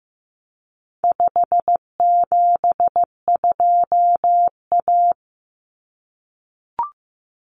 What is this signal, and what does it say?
Morse code, heard as a keyed tone, about 15 wpm, 700 Hz. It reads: 572A
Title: Common Cut Numbers - 4-Digit 15wpm